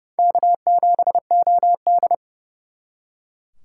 Morse code, heard as a keyed tone, about 40 wpm, 700 Hz. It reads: K7OD